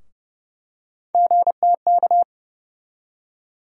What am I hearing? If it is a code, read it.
Morse code, heard as a keyed tone, about 30 wpm, 700 Hz. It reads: GTK
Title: USA Call Sign Postfixes - 3 Letters 30wpm